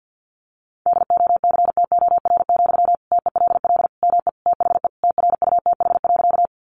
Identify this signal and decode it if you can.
Morse code, heard as a keyed tone, about 50 wpm, 700 Hz. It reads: DOCTOR, TELL ME THE TRUTH.
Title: Sentences from Top 1500 Words 50wpm